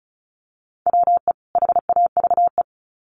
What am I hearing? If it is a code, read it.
Morse code, heard as a keyed tone, about 34 wpm, 700 Hz. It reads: WE HAVE
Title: Sets of 2 Words 35wpm